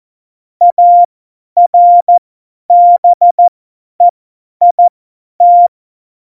Morse code, heard as a keyed tone, about 14 wpm, spaced as for 10 wpm, 700 Hz. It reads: ARBEIT
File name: Morsekurs / Witze / Witz 22. Oktober 2019, 50 F70